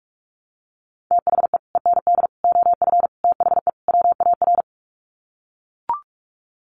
Morse code, heard as a keyed tone, about 44 wpm, 700 Hz. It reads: THE END OF THE WAR
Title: Sets of 5 Words 45wpm